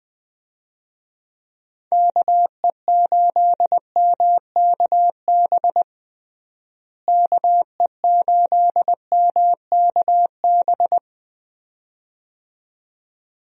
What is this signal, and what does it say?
Morse code, heard as a keyed tone, about 20 wpm, 700 Hz. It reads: KE8MKB KE8MKB
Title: USA Call Signs - 6 Characters - Repeated 2x - 20wpm